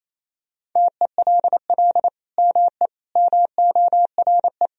Morse code, heard as a keyed tone, about 28 wpm, 700 Hz. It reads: TELL ME MORE
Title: Sentences from Top 1000 Words - Review Part 4 - 28wpm